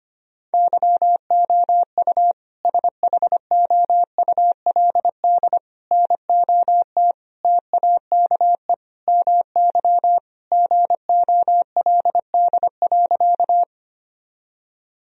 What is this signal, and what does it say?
Morse code, heard as a keyed tone, about 25 wpm, 700 Hz. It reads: YOU SHOULD NOT TAKE MY GOLD.
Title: Sentences from Top 1000 Words - Review Part 3 - 25wpm